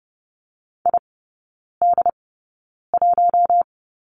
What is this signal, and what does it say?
Morse code, heard as a keyed tone, about 30 wpm, spaced as for 9 wpm, 700 Hz. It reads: ID1
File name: course / 30:10wpm / 087 Lesson 15e - Number 5 - Three Characters 30wpm